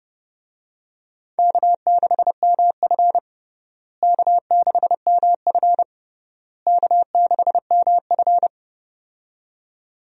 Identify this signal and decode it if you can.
Morse code, heard as a keyed tone, about 30 wpm, 700 Hz. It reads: K6MF K6MF K6MF
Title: USA Call Signs - 4 Characters - Repeated 3x - 30wpm